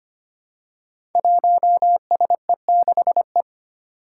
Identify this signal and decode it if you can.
Morse code, heard as a keyed tone, about 25 wpm, 700 Hz. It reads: 1SE6E
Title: Five Challenging Letter-Number Combinations 25wpm